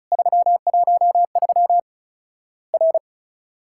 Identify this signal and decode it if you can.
Morse code, heard as a keyed tone, about 36 wpm, 700 Hz. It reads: 313 R
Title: Worked All Europe DX Contest - QTC Exchange - 35wpm